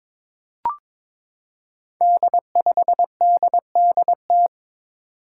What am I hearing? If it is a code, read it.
Morse code, heard as a keyed tone, about 22 wpm, 700 Hz. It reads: D5DDT
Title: Five Challenging Letter-Number Combinations 22wpm